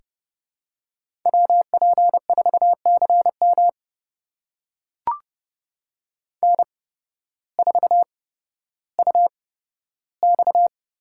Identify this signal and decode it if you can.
Morse code, heard as a keyed tone, about 29 wpm, spaced as for 14 wpm, 700 Hz. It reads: WP4CM N4UX
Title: USA Call Signs - 5 Characters - 8x Character Spacing - 30wpm